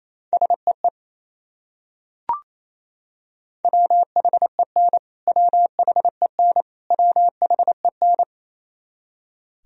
Morse code, heard as a keyed tone, about 28 wpm, 700 Hz. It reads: SEE WHEN WHEN WHEN
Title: Top 100 Words Repeated 28wpm